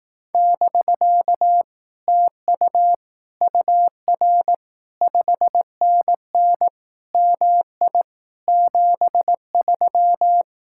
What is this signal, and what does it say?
Morse code, heard as a keyed tone, about 18 wpm, 700 Hz. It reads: <BK> TU UR 5NN MI 73
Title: Speed-Racing Common QSO Phrases Encore - 15wpm